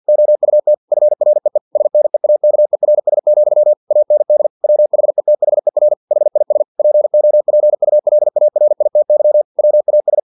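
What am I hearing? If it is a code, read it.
Morse code, heard as a keyed tone, about 50 wpm, 600 Hz. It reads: OUT FREE SNEAKERS, AND WHETHER HIS POPULARITY WAS
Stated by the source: Morse practice text from an English-language novel at 49 wpm.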